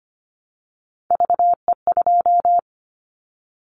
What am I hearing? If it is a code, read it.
Morse code, heard as a keyed tone, about 25 wpm, 700 Hz. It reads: VE2
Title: Callsign Prefixes for Top 50 DXCC - 25wpm